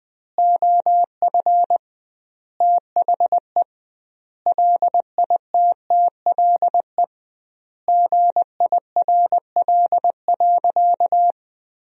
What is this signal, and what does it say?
Morse code, heard as a keyed tone, about 21 wpm, 700 Hz. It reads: OF THE LITTLE GIRL.
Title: 2x Word-Spacing - Sentences from Top 400 Words - 20wpm